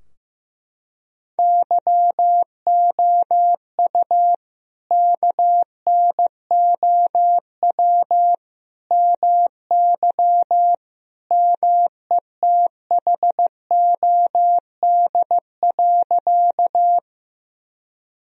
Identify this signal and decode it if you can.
Morse code, heard as a keyed tone, about 15 wpm, 700 Hz. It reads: YOU KNOW MY METHOD.
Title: Sentences from Top 1000 Words - Review Part 3 - 15wpm